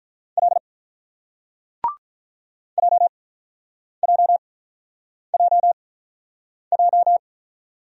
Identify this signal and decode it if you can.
Morse code, heard as a keyed tone, about 46 wpm, spaced as for 9 wpm, 700 Hz. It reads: L JJJJ